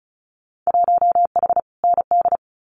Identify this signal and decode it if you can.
Morse code, heard as a keyed tone, about 36 wpm, 700 Hz. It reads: 1H ND
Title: Winter Field Day Exchanges 35wpm